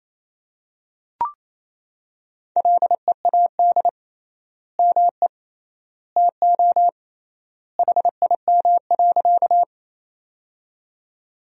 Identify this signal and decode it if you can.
Morse code, heard as a keyed tone, about 28 wpm, 700 Hz. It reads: LEAD ME TO HIM.
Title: 3x Word-Spacing - Sentences from Top 500 Words - 28wpm